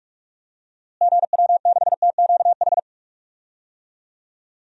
Morse code, heard as a keyed tone, about 45 wpm, 700 Hz. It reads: GW6TQH
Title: DXCC-100 Call Signs - 6 Characters - Repeated 2x - 45wpm